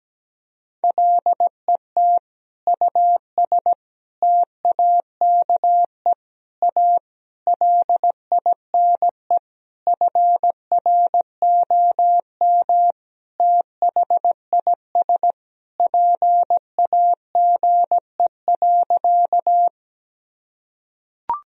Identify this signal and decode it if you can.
Morse code, heard as a keyed tone, about 17 wpm, 700 Hz. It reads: LET US TAKE A LINE FROM THIS PAGE.
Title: Sentences from Top 1000 Words - Review Part 1 - 17wpm